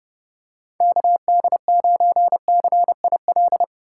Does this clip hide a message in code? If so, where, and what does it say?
Yes: Morse code from 0.8 to 3.6 s, heard as a keyed tone, about 31 wpm, 700 Hz. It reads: KD9CIL